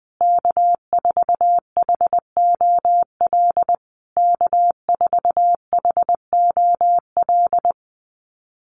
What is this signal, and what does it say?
Morse code, heard as a keyed tone, about 20 wpm, 700 Hz. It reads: K4HOL K4HOL
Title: USA Call Signs Repeated 3x 20wpm